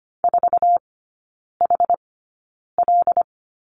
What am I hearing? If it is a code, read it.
Morse code, heard as a keyed tone, about 25 wpm, spaced as for 9 wpm, 700 Hz. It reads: 4HL
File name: Lesson 17e - Letter u - Three Characters 25wpm